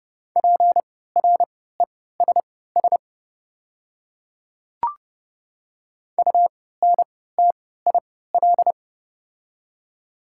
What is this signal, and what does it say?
Morse code, heard as a keyed tone, about 30 wpm, spaced as for 17 wpm, 690 Hz. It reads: PRESS UNTIL
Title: 3x Character Spacing - Top 300 Words - 30wpm